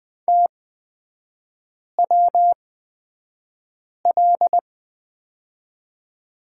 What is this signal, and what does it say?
Morse code, heard as a keyed tone, about 20 wpm, spaced as for 5 wpm, 700 Hz. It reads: TWL
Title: 255 Lesson 41e - Period - Three Characters 20wpm